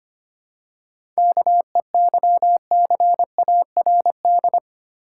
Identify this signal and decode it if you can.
Morse code, heard as a keyed tone, about 25 wpm, 690 Hz. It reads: KEYCARD
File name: practice / 25wpm / Easy Words with Hints - 25wpm